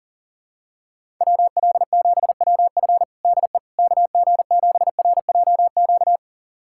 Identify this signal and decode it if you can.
Morse code, heard as a keyed tone, about 40 wpm, 700 Hz. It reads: WL7WF DE KG7RJQ
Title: USA Callsign DE Callsign 40wpm